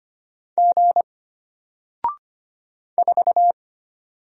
Morse code, heard as a keyed tone, about 25 wpm, 700 Hz. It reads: G 4
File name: Single-Letter-Number-25wpm